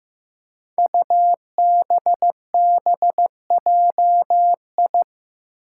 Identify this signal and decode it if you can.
Morse code, heard as a keyed tone, about 15 wpm, 700 Hz. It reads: UBBJI
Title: Five Challenging Letter-Number Combinations 15wpm